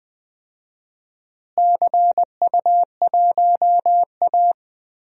Morse code, heard as a keyed tone, about 21 wpm, 700 Hz. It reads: CU1A
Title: DXCC-100 Call Signs - 4 Characters - Repeated 2x - 20wpm